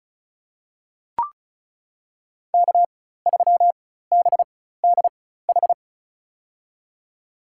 Morse code, heard as a keyed tone, about 36 wpm, spaced as for 16 wpm, 700 Hz. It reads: K3BDH
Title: USA Call Signs - 5 Characters - 4x Character Spacing - 35wpm